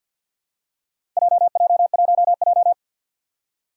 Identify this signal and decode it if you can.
Morse code, heard as a keyed tone, about 51 wpm, 700 Hz. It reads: JJ1J
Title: Four Challenging Letter-Number Combinations 50wpm